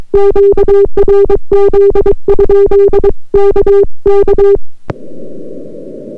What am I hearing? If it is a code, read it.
Morse code, heard as a keyed tone, about 22 wpm, 400 Hz. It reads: QRZ? KK